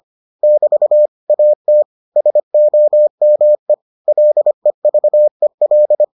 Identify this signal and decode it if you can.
Morse code, heard as a keyed tone, about 25 wpm, 600 Hz. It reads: <BT> AT SOME LEVEL